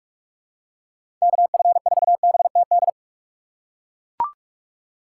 Morse code, heard as a keyed tone, about 42 wpm, 700 Hz. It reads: KF4BTD
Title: USA Call Signs - 2x Character Spacing - 45wpm